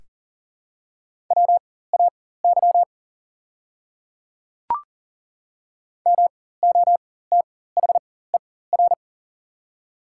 Morse code, heard as a keyed tone, about 40 wpm, spaced as for 19 wpm, 700 Hz. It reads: WAY MOTHER
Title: 4x Character Spacing - Top 200 Words - 40wpm